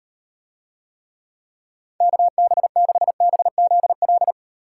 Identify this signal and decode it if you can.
Morse code, heard as a keyed tone, about 38 wpm, 700 Hz. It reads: KB6BZL